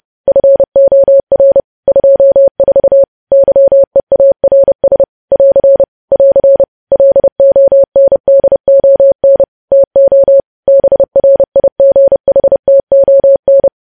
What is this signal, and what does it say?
Morse code, heard as a keyed tone, about 30 wpm, 550 Hz. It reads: FOR 24 YEARS <AR> <AR> LONDON TO BRIGHTON
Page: news headlines in morse code at 30 wpm